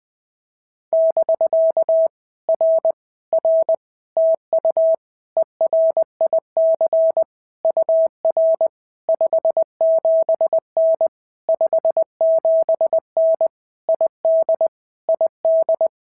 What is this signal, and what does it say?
Morse code, heard as a keyed tone, about 20 wpm, 650 Hz. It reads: <BK> R R TU ERIC UR 57N 57N ID ID
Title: POTA QSO Practice - 20wpm